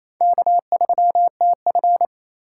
Morse code, heard as a keyed tone, about 28 wpm, 700 Hz. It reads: K3TF
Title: K1USN SST QSO Practice - 28wpm